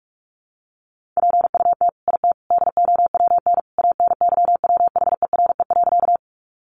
Morse code, heard as a keyed tone, about 44 wpm, 700 Hz. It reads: PUT IT DOWN ANYWHERE.
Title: Sentences from Top 1400 Words 45wpm